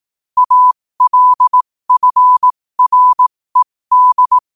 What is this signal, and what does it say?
Morse code, heard as a keyed tone, about 18 wpm, spaced as for 15 wpm, 1000 Hz. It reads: ALFRED
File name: qotdfiles / 15wpm